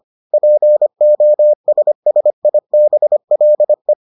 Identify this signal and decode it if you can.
Morse code, heard as a keyed tone, about 25 wpm, 600 Hz. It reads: POSSIBLE